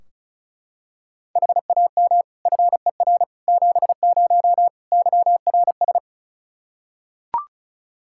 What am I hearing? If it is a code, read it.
Morse code, heard as a keyed tone, about 34 wpm, 700 Hz. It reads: HAM FER 70 YRS
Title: Practice QSO - Exchange 3 - More Info - 35wpm